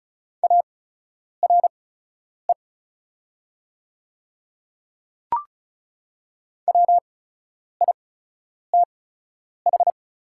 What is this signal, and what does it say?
Morse code, heard as a keyed tone, about 36 wpm, spaced as for 10 wpm, 700 Hz. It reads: ARE WITH
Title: 8x Character Spacing - Top 100 Words - 35wpm